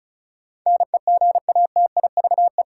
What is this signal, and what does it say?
Morse code, heard as a keyed tone, about 36 wpm, 700 Hz. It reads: NEGATIVE
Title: Sentences from Top 1000 Words - New Word Drill - 35wpm